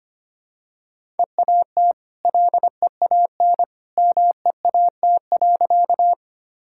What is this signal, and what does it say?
Morse code, heard as a keyed tone, about 25 wpm, 700 Hz. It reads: EAT LEAN MEAT.
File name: Easy Sentences with Hints - 25wpm